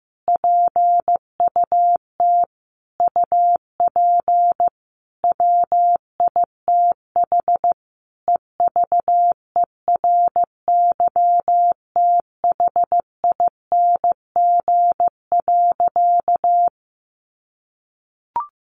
Morse code, heard as a keyed tone, about 15 wpm, 700 Hz. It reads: PUT UP WITH EVERYTHING.
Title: Sentences from Top 1100 Words 15wpm